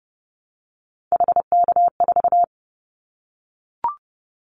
Morse code, heard as a keyed tone, about 30 wpm, 700 Hz. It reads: HK4
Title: Callsign Prefixes for Top 100 DXCC - 30wpm